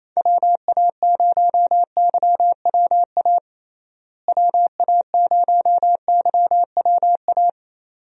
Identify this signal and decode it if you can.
Morse code, heard as a keyed tone, about 28 wpm, 700 Hz. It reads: WA0YWA WA0YWA
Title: USA Call Signs - 6 Characters - Repeated 2x - 28wpm